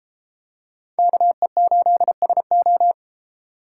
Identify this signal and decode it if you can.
Morse code, heard as a keyed tone, about 33 wpm, 700 Hz. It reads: KE8SO